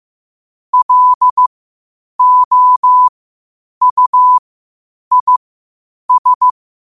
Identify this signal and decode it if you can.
Morse code, heard as a keyed tone, about 15 wpm, spaced as for 8 wpm, 1000 Hz. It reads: LOUIS
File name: qotdfiles / 8wpm